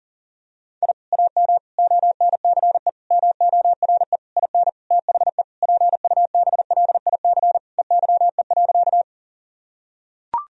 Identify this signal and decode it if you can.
Morse code, heard as a keyed tone, about 40 wpm, 700 Hz. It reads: I AM ONCE MORE IN THE PUBLIC EYE.
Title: Sentences from Top 1000 Words - Review Part 3 - 40wpm